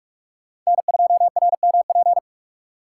Morse code, heard as a keyed tone, about 46 wpm, 700 Hz. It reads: N2RMP
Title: USA Call Signs - 6x Character Spacing - 45wpm